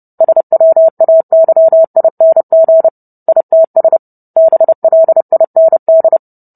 Morse code, heard as a keyed tone, about 30 wpm, 650 Hz. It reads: SWAYING ITS BLIND